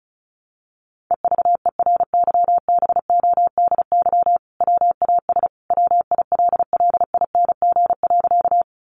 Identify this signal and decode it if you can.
Morse code, heard as a keyed tone, about 34 wpm, 700 Hz. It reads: EVERYBODY WAS WILLING.
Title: Sentences from Top 1400 Words 35wpm